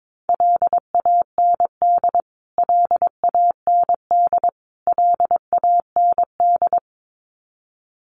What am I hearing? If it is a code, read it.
Morse code, heard as a keyed tone, about 22 wpm, 700 Hz. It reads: LAND LAND LAND
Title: Top 200 Words Repeated 22wpm